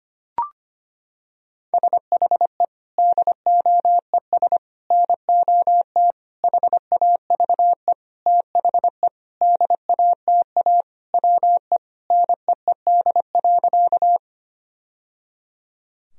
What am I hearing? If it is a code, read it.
Morse code, heard as a keyed tone, about 25 wpm, 700 Hz. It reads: SHE DOES NOT HAVE THE DATA WE NEED.
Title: Sentences from Top 1000 Words - Review Part 3 - 25wpm